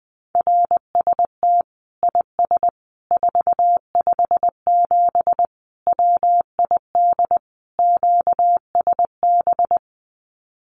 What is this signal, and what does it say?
Morse code, heard as a keyed tone, about 20 wpm, 700 Hz. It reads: RST IS 457 WID QSB
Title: Common QSO Phrases 20wpm